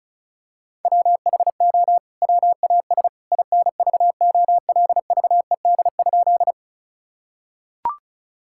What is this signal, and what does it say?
Morse code, heard as a keyed tone, about 34 wpm, 700 Hz. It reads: WHO WAS INVOLVED?
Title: Sentences from Top 1000 Words - Review Part 5 - 35wpm